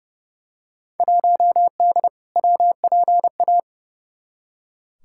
Morse code, heard as a keyed tone, about 30 wpm, 700 Hz. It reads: 1D WPA
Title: Field Day Exchanges - 30wpm